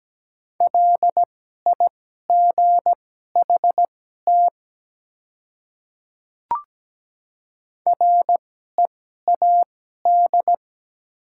Morse code, heard as a keyed tone, about 17 wpm, spaced as for 12 wpm, 700 Hz. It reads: LIGHT READ
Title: 2x Character Spacing - Top 300 Words - 17wpm